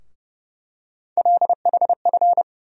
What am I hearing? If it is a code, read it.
Morse code, heard as a keyed tone, about 30 wpm, 700 Hz. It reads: LHF
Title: USA Call Sign Postfixes - 3 Letters 30wpm